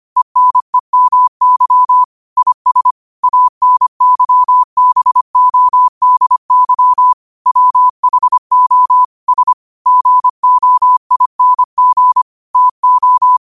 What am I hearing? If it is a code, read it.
Morse code, heard as a keyed tone, about 25 wpm, 1000 Hz. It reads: ENEMY IS ANYBODY WHO S GOING TO